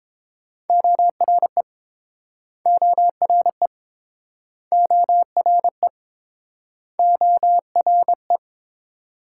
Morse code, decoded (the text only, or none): ORE ORE ORE ORE